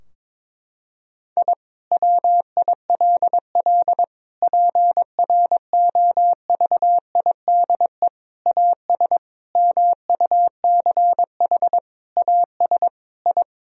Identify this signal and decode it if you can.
Morse code, heard as a keyed tone, about 22 wpm, 700 Hz. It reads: I WILL PROVIDE AS MUCH AS I
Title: Sentences from Top 1000 Words - Review Part 3 - 22wpm